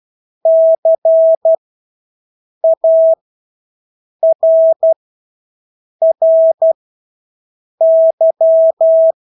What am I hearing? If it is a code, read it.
Morse code, heard as a keyed tone, about 12 wpm, spaced as for 6 wpm, 650 Hz. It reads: CARRY
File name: Engelske ord 12-6 wpm